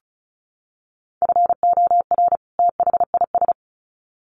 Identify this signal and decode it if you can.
Morse code, heard as a keyed tone, about 34 wpm, 700 Hz. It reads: FOR THIS
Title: Sets of 2 Words 35wpm